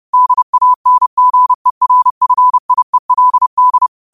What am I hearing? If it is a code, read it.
Morse code, heard as a keyed tone, about 30 wpm, 1000 Hz. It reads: DANGERFIELD